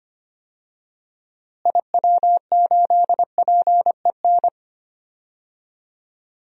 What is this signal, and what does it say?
Morse code, heard as a keyed tone, about 25 wpm, 700 Hz. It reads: IW8PEN
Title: DXCC-100 Call Signs - 6 Characters - Repeated 2x - 25wpm